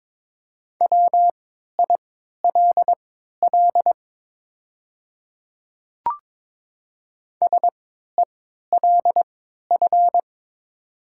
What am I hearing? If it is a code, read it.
Morse code, heard as a keyed tone, about 22 wpm, spaced as for 13 wpm, 700 Hz. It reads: WILL SELF
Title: 3x Character Spacing - Top 200 Words - 22wpm